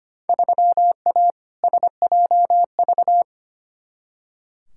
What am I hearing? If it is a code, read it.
Morse code, heard as a keyed tone, about 25 wpm, 700 Hz. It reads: 3A SJV
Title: Field Day Exchanges Encore - 25wpm